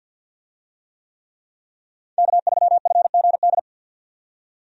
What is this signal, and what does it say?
Morse code, heard as a keyed tone, about 51 wpm, 700 Hz. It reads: K3FGD
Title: USA Call Signs - 5 Characters - 50wpm